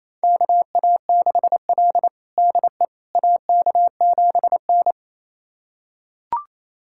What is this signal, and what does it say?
Morse code, heard as a keyed tone, about 28 wpm, 700 Hz. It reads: KA6L DE AK7N
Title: USA Callsign DE Callsign - 4 Characters - 28wpm